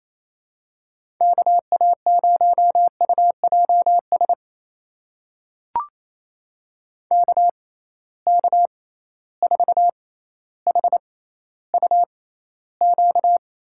Morse code, heard as a keyed tone, about 28 wpm, spaced as for 14 wpm, 700 Hz. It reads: KA0UJS KK4HUQ
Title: USA Call Signs - 6 Characters - 6x Character Spacing - 28wpm